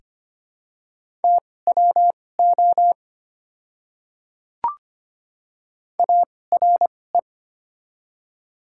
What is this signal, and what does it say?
Morse code, heard as a keyed tone, about 25 wpm, spaced as for 18 wpm, 690 Hz. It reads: TWO ARE
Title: 2x Character Spacing - Top 200 Words - 25wpm